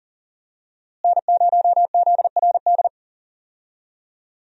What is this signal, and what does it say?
Morse code, heard as a keyed tone, about 40 wpm, 700 Hz. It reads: N0ZRD
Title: USA Call Signs - 5 Characters - Repeated 2x - 40wpm